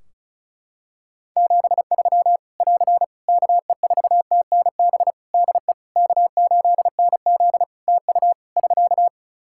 Morse code, heard as a keyed tone, about 36 wpm, 700 Hz. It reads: 73 <AR> KE4TNB DE K8NZ TU <SK>
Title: Practice QSO - Exchange 4 - Ending - 35wpm